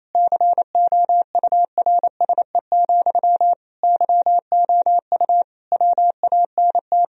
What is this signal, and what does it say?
Morse code, heard as a keyed tone, about 28 wpm, 700 Hz. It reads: COURSE, YOU WANT
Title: Sentences from Top 1000 Words - Review Part 1 - 28wpm